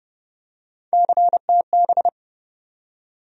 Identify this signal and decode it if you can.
Morse code, heard as a keyed tone, about 30 wpm, 700 Hz. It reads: CTB
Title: USA Call Sign Postfixes - 3 Letters 30wpm